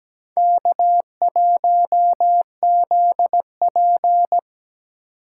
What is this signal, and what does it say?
Morse code, heard as a keyed tone, about 17 wpm, 700 Hz. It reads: K1ZP
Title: USA Call Signs - 4 Characters - 4x Character Spacing - 17wpm